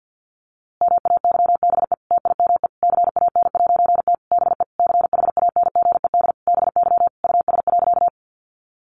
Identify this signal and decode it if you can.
Morse code, heard as a keyed tone, about 50 wpm, 700 Hz. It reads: MAYBE TIME CAN'T BE CHANGED BY US.
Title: A Sound Of Thunder 50wpm